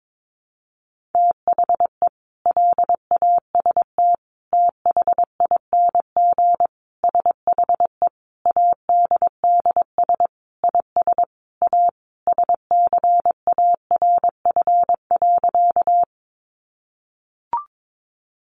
Morse code, heard as a keyed tone, about 22 wpm, 700 Hz. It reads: THE LAST THING SHE ADDS IS A SCARF.
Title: Easy Phrases 22wpm